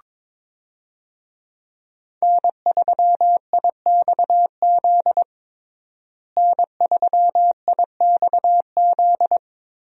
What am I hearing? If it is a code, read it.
Morse code, heard as a keyed tone, about 22 wpm, 700 Hz. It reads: N3IXZ N3IXZ
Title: USA Call Signs - 5 Characters - Repeated 2x - 22wpm